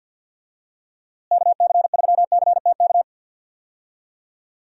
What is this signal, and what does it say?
Morse code, heard as a keyed tone, about 52 wpm, 700 Hz. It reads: KC3CTK
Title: USA Call Signs - 6 Characters - Repeated 3x - 50wpm